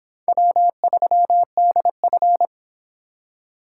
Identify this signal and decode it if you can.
Morse code, heard as a keyed tone, about 26 wpm, 700 Hz. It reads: W3DF